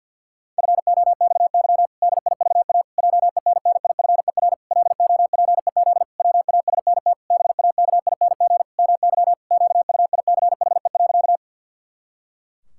Morse code, heard as a keyed tone, about 50 wpm, 700 Hz. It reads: FOXY DIVA JENNIFER LOPEZ WASNT BAKING MY QUICHE.
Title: Pangram Challenge v2 - 50wpm